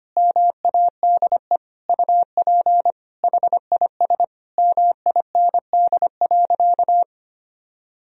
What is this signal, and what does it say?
Morse code, heard as a keyed tone, about 25 wpm, 700 Hz. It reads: MADE UP HIS MIND.
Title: Sentences from Top 1000 Words - Review Part 2 - 25wpm